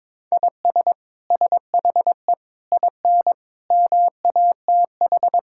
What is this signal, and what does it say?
Morse code, heard as a keyed tone, about 22 wpm, 700 Hz. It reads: IS SHE IN MATH
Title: Sentences from Top 1000 Words - Review Part 6 - 22wpm